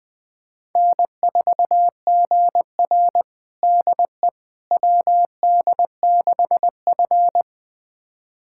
Morse code, heard as a keyed tone, about 20 wpm, 700 Hz. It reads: N4GR DE WD6F
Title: USA Callsign DE Callsign - 4 Characters - 20wpm